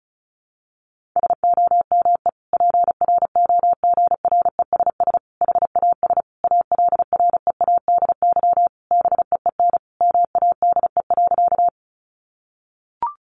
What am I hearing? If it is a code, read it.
Morse code, heard as a keyed tone, about 34 wpm, 700 Hz. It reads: SOME PROGRESS HAS ALREADY BEEN MADE.
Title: Sentences from Top 1400 Words 35wpm